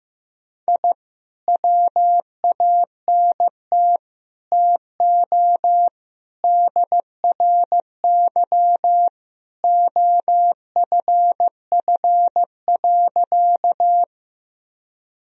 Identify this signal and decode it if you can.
Morse code, heard as a keyed tone, about 15 wpm, 700 Hz. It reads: I WANT TO DRY OFF.
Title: Sentences from Top 1000 Words - Review Part 3 - 15wpm